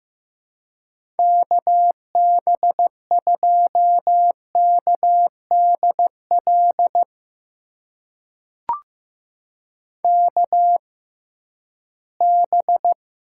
Morse code, heard as a keyed tone, about 15 wpm, spaced as for 11 wpm, 700 Hz. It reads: KB2KDL KB
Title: USA Call Signs - 6 Characters - 6x Character Spacing - 15wpm